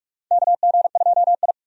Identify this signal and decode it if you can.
Morse code, heard as a keyed tone, about 46 wpm, 700 Hz. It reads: KG2I
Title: USA Call Signs - 4 Characters - 45wpm